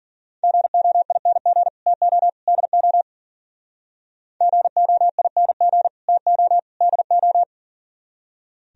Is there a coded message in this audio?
Yes: Morse code from 0.4 to 7.4 s, heard as a keyed tone, about 53 wpm, 700 Hz. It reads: GOING TO DO GOING TO DO